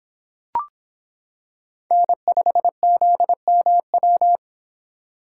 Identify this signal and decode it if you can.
Morse code, heard as a keyed tone, about 25 wpm, 700 Hz. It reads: N5ZMW